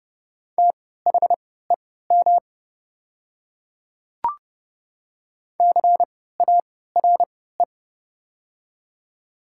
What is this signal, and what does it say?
Morse code, heard as a keyed tone, about 30 wpm, spaced as for 17 wpm, 700 Hz. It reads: THEM CARE
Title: 3x Character Spacing - Top 300 Words - 30wpm